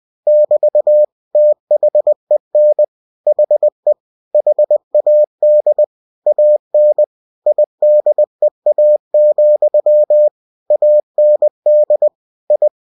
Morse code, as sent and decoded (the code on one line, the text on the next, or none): -...- / - .... -. / .... / .... .- -.. / .- -. / .. -.. .- --..-- / .- -. -.. / ..
<BT> THEN HE HAD AN IDEA, AND I